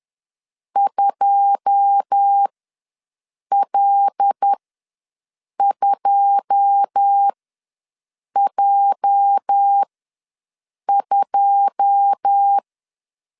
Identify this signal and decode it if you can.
Morse code, heard as a keyed tone, about 10 wpm, spaced as for 6 wpm, 800 Hz. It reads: 2L2J2
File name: LEZIONI / 9 2 ws 10wpm 4130